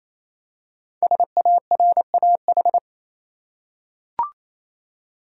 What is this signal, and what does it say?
Morse code, heard as a keyed tone, about 28 wpm, 700 Hz. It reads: SARAH
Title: USA Common Names - 28wpm